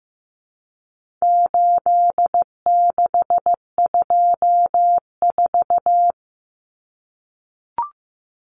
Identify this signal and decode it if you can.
Morse code, heard as a keyed tone, about 15 wpm, 700 Hz. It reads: 8624
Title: Common Cut Numbers - 4-Digit 15wpm